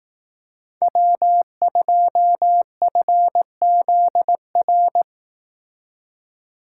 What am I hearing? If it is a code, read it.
Morse code, heard as a keyed tone, about 18 wpm, 700 Hz. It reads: W2FZR
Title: Speed Racing - USA 1x3 Call Signs 15wpm